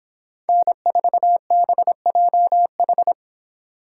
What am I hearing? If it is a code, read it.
Morse code, heard as a keyed tone, about 24 wpm, 700 Hz. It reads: N4BJH